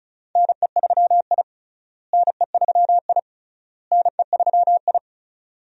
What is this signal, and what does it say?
Morse code, heard as a keyed tone, about 36 wpm, 700 Hz. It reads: NE3I NE3I NE3I
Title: USA Call Signs - 4 Characters - Repeated 3x - 35wpm